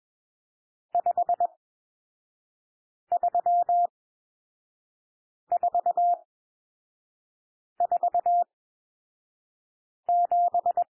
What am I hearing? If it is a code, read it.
Morse code, heard as a keyed tone, about 21 wpm, spaced as for 5 wpm, 700 Hz. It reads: 53447
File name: cwmp3 / N1 SOUND NUMBERS 21 wpm